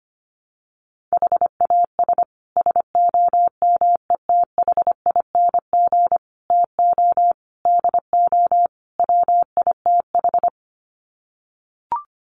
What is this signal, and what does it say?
Morse code, heard as a keyed tone, about 25 wpm, 700 Hz. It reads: HAS SOMETHING TO DO WITH
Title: Sets of 5 Words Encore 25wpm